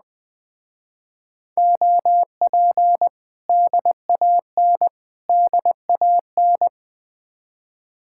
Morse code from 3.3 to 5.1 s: DAN

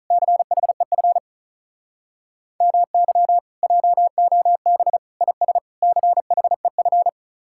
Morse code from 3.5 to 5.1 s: JOB

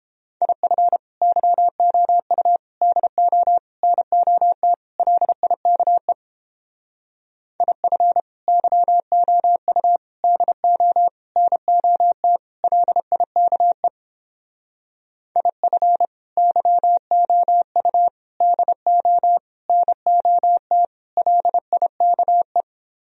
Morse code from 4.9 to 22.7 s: LIKE IF YOU DO NOT LIKE IF YOU DO NOT LIKE

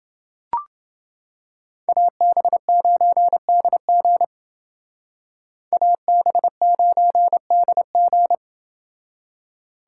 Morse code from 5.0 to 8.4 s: AB9DG

